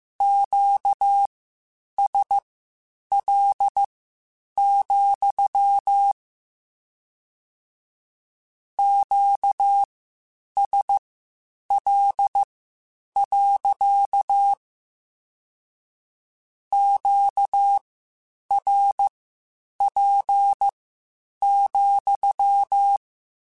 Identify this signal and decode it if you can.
Morse code, heard as a keyed tone, about 15 wpm, spaced as for 8 wpm, 800 Hz. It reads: QSL, QSL. QRP,
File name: Q Calls / Q Calls-mp3s / Q Calls sppc @13